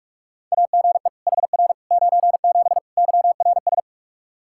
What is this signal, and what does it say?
Morse code, heard as a keyed tone, about 46 wpm, 700 Hz. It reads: AGE HR 97 YRS
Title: Practice QSO - Exchange 3 - More Info - 45wpm